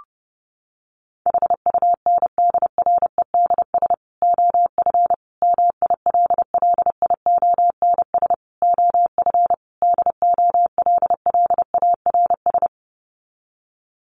Morse code, heard as a keyed tone, about 30 wpm, 700 Hz. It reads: HUNDREDS OF MILLIONS OF DOLLARS
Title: Sets of 5 Words 30wpm